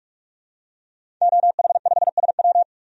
Morse code, heard as a keyed tone, about 44 wpm, 700 Hz. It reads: OH5SW
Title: DXCC-100 Call Signs - 5 Characters - 45wpm